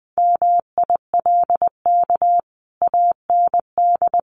Morse code, heard as a keyed tone, about 20 wpm, 700 Hz. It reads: MILK AND
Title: Sentences from Top 1500 Words 20wpm